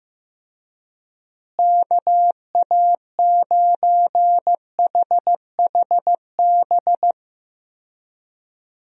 Morse code, heard as a keyed tone, about 15 wpm, 700 Hz. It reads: KA9HHB
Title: USA Call Signs - 6 Characters - Repeated 3x - 15wpm